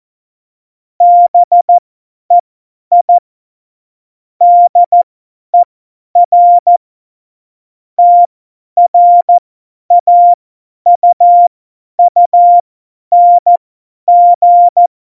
Morse code, heard as a keyed tone, about 14 wpm, spaced as for 10 wpm, 700 Hz. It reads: BEI DER TRAUUNG
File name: Morsekurs / Witze / Witz 13. August 2019, 50 F70